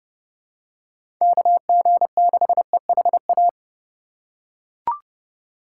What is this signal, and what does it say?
Morse code, heard as a keyed tone, about 30 wpm, 700 Hz. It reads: KG6EHA
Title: USA Call Signs - 6 Characters - 4x Character Spacing - 30wpm